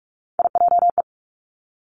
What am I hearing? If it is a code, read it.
Morse code, heard as a keyed tone, about 46 wpm, 700 Hz. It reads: IJE
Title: Three Challenging Letter-Number Combinations 45wpm